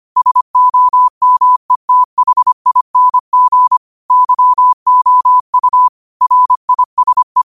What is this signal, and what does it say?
Morse code, heard as a keyed tone, about 25 wpm, 1000 Hz. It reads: SOMETHING YOU RISE
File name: qotdfiles / 25wpm